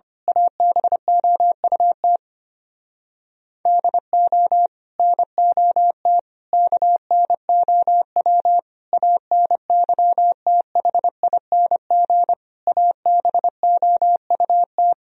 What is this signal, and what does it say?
Morse code, heard as a keyed tone, about 24 wpm, 700 Hz. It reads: ABOUT DO NOT KNOW ANYTHING ABOUT
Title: Speed-Racing - Sets of 5 Words 25wpm